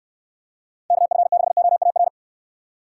Morse code, heard as a keyed tone, about 69 wpm, 700 Hz. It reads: KF6OAR